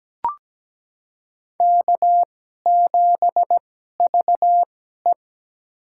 Morse code, heard as a keyed tone, about 17 wpm, spaced as for 12 wpm, 700 Hz. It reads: K7VE
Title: USA Call Signs - 4 Characters - 2x Character Spacing - 17wpm